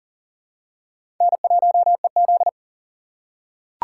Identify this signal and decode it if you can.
Morse code, heard as a keyed tone, about 40 wpm, 690 Hz. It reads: N1EZ
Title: USA Call Signs - 2x Character Spacing - 40wpm